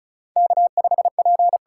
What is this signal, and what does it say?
Morse code, heard as a keyed tone, about 36 wpm, 700 Hz. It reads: K5P
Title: USA Call Signs - 4 Characters - 8x Character Spacing - 35wpm